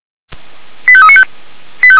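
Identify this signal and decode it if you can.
Five-tone selective call (ZVEI), heard as quick runs of five tones, tones of about 70 ms. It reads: 85395
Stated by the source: ZVEI five-tone call received by SDR, from the feuerwehr-orlamuende.org fire brigade audio server.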